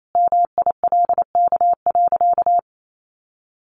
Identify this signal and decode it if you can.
Morse code, heard as a keyed tone, about 28 wpm, 700 Hz. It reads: MILK.
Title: Sentences from Top 1500 Words 28wpm